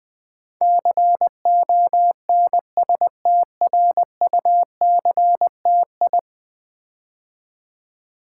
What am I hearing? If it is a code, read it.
Morse code, heard as a keyed tone, about 20 wpm, 700 Hz. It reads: CONSTRUCTI
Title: Build A Word - 5 Letters At A Time - 12-Letter Words - 20wpm